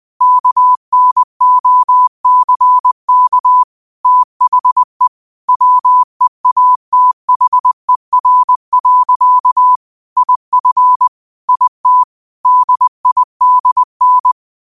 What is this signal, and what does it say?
Morse code, heard as a keyed tone, about 20 wpm, 1000 Hz. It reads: KNOCK THE WEATHER. IF IT DIDN
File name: qotdfiles / 20wpm